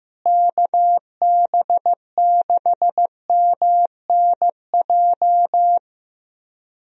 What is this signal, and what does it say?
Morse code, heard as a keyed tone, about 15 wpm, 700 Hz. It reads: KB6MNJ
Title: USA Call Signs - 6 Characters - Repeated 2x - 15wpm